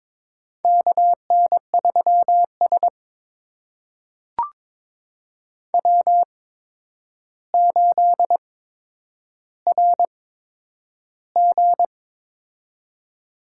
Morse code, heard as a keyed tone, about 22 wpm, spaced as for 9 wpm, 700 Hz. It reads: KN3S W8RG
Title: USA Call Signs - 4 Characters - 8x Character Spacing - 22wpm